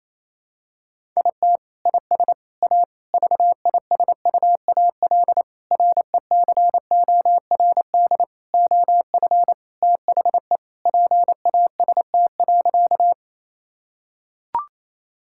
Morse code, heard as a keyed tone, about 27 wpm, 700 Hz. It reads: IT IS A VISUAL RECORD OF THE PAST.
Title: Sentences from Top 1000 Words - Review Part 6 - 28wpm